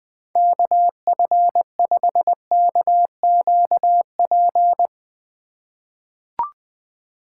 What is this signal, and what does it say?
Morse code, heard as a keyed tone, about 20 wpm, 700 Hz. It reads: KF5KQP
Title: USA Call Signs - 6 Characters - 4x Character Spacing - 20wpm